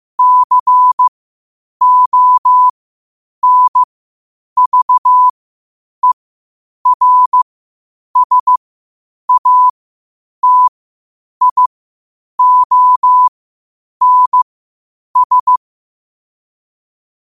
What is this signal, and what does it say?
Morse code, heard as a keyed tone, about 15 wpm, spaced as for 8 wpm, 1000 Hz. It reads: CONVERSATIONS
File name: qotdfiles / 8wpm